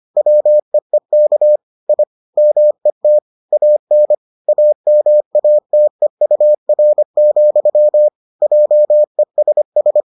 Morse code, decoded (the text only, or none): WEEK I MET AN AMATEUR, JESS